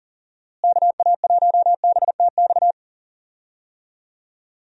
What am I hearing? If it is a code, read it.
Morse code, heard as a keyed tone, about 40 wpm, 700 Hz. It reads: KA1BTX
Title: USA Call Signs - 6 Characters - Repeated 2x - 40wpm